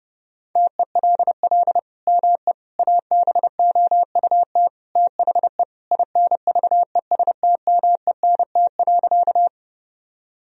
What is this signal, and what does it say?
Morse code, heard as a keyed tone, about 31 wpm, 700 Hz. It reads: TELL ME ABOUT THE INVESTMENT.
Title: Sentences from Top 1000 Words - Review Part 4 - 30wpm